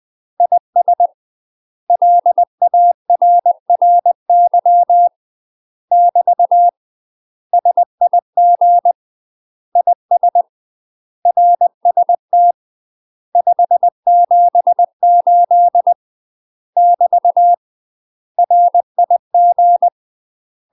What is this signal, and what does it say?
Morse code, heard as a keyed tone, about 20 wpm, 700 Hz. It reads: IS LARRY <BT> SIG IS RST 578 <BT> RIG
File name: UK MASTER QSOs TRAINING / 20WPM / UK MASTER QSOs 0049